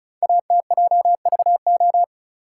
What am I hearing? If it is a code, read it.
Morse code, heard as a keyed tone, about 36 wpm, 690 Hz. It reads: ATJVO
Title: Code Groups - 5 Letters - 35wpm